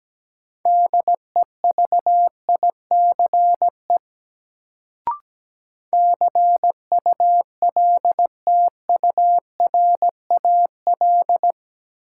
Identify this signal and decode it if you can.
Morse code, heard as a keyed tone, about 17 wpm, 700 Hz. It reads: DEVICE CULTURAL
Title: Sentences from Top 1500 Words - New Word Drill - 17wpm